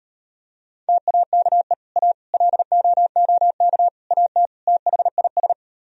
Morse code, heard as a keyed tone, about 39 wpm, 700 Hz. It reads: TAKE A LOOK AT THIS